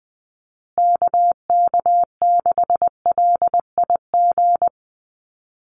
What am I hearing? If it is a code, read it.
Morse code, heard as a keyed tone, about 20 wpm, 700 Hz. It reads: KK6LIG
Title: USA Call Signs Repeated 3x 20wpm